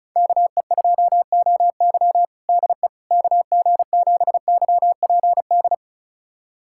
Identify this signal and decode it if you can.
Morse code, heard as a keyed tone, about 34 wpm, 700 Hz. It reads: KE2OY DE KG7YPD
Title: USA Callsign DE Callsign 35wpm